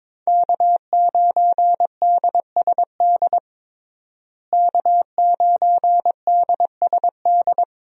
Morse code, heard as a keyed tone, about 22 wpm, 700 Hz. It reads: K9DSD K9DSD
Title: USA Call Signs - 5 Characters - Repeated 3x - 22wpm